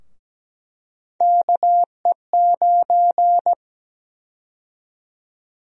Morse code, heard as a keyed tone, about 17 wpm, 700 Hz. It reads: KE9
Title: USA Call Sign Prefixes - Repeated 2x - 17wpm